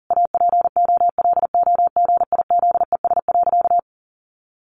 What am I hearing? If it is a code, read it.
Morse code, heard as a keyed tone, about 40 wpm, 700 Hz. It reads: APOLOGIZES.
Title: Additional Easy Phrases 40wpm